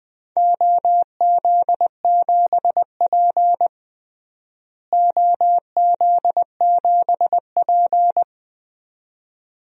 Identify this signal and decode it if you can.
Morse code, heard as a keyed tone, about 21 wpm, 700 Hz. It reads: OZ7P OZ7P
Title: DXCC-100 Call Signs - 4 Characters - Repeated 2x - 20wpm